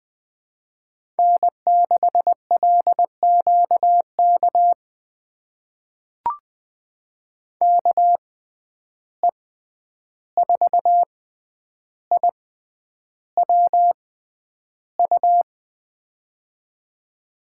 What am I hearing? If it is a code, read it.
Morse code, heard as a keyed tone, about 20 wpm, spaced as for 10 wpm, 700 Hz. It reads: N6LQK KE4IWU
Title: USA Call Signs - 6x Character Spacing - 20wpm